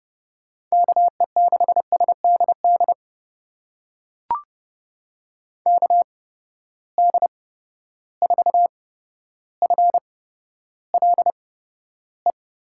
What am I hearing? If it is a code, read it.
Morse code, heard as a keyed tone, about 32 wpm, spaced as for 13 wpm, 700 Hz. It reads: KE6SDD KD4FLE
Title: USA Call Signs - 6 Characters - 8x Character Spacing - 30wpm